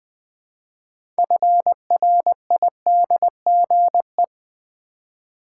Morse code, heard as a keyed tone, about 20 wpm, 700 Hz. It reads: FRIDGE
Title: Build A Word - 2 Letters At A Time - 6-Letter Words - 20wpm